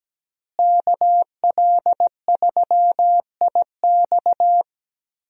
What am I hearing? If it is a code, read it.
Morse code, heard as a keyed tone, about 17 wpm, 700 Hz. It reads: KL3IX
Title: USA Call Signs - 5 Characters - Repeated 3x - 17wpm